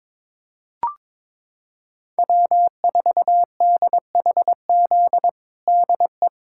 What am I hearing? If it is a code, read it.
Morse code, heard as a keyed tone, about 21 wpm, 700 Hz. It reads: W4DHZ DE
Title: USA Callsign DE Callsign - 5 Characters - 22wpm